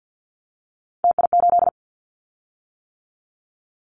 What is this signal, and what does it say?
Morse code, heard as a keyed tone, about 50 wpm, 690 Hz. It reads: TI8
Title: Callsign Prefixes for Top 100 DXCC - 50wpm